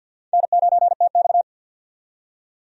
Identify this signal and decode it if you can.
Morse code, heard as a keyed tone, about 52 wpm, 700 Hz. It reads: N9TX
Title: USA Call Signs - 4 Characters - 50wpm